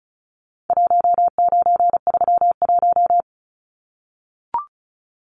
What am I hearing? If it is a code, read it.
Morse code, heard as a keyed tone, about 34 wpm, 700 Hz. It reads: 1931
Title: Numbers - 4-Digit 35wpm